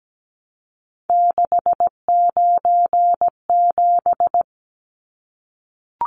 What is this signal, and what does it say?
Morse code, heard as a keyed tone, about 17 wpm, 700 Hz. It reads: 697
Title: Numbers - 3-Digit 17wpm